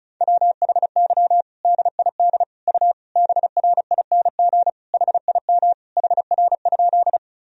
Morse code, heard as a keyed tone, about 36 wpm, 700 Hz. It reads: WHY DID U BRING HIM HR?
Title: Sentences from Top 500 Words and Abbreviations 35wpm